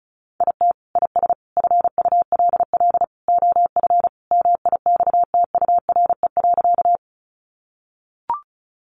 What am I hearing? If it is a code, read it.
Morse code, heard as a keyed tone, about 34 wpm, 700 Hz. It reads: IT IS FULL OF MIXTURE.
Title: Sentences from Top 1500 Words 35wpm